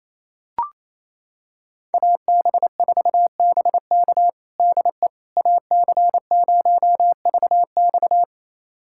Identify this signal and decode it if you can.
Morse code, heard as a keyed tone, about 28 wpm, 700 Hz. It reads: AB4BK DE AC0VX
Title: USA Callsign DE Callsign - 5 Characters - 28wpm